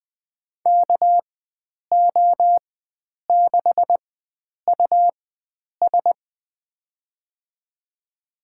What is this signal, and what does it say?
Morse code, heard as a keyed tone, about 21 wpm, spaced as for 9 wpm, 700 Hz. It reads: KO6US
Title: USA Call Signs - 4x Character Spacing - 20wpm